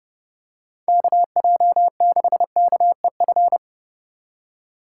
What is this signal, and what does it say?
Morse code, heard as a keyed tone, about 32 wpm, 700 Hz. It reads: KJ6KEF
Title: USA Call Signs - 6 Characters - 2x Character Spacing - 30wpm